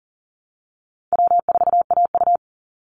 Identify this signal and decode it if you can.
Morse code, heard as a keyed tone, about 40 wpm, 700 Hz. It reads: W4AU
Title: Easy Call Signs - 40wpm